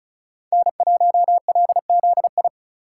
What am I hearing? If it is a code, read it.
Morse code, heard as a keyed tone, about 36 wpm, 700 Hz. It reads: N1LZI
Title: USA Call Signs - 5 Characters - 3x Character Spacing - 35wpm